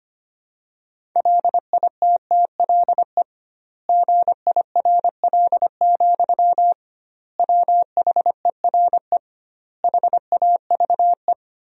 Morse code, heard as a keyed tone, about 25 wpm, 700 Hz. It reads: LITTLE GIRL, WHERE HAVE
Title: 2x Word-Spacing - Sentences from Top 400 Words - 25wpm